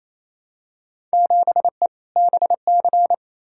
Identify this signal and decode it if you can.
Morse code, heard as a keyed tone, about 28 wpm, 700 Hz. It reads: 7E BC
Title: Field Day Exchanges Encore - 28wpm